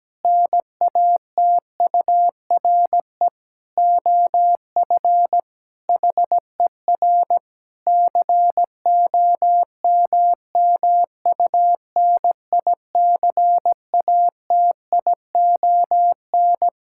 Morse code, heard as a keyed tone, about 17 wpm, 700 Hz. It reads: NATURE OF HER COMMUNICATION
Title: Sentences from Top 1000 Words - Review Part 6 - 17wpm